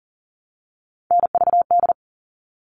Morse code, heard as a keyed tone, about 40 wpm, 700 Hz. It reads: NVD
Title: All Cut Numbers - 3-Digit 40wpm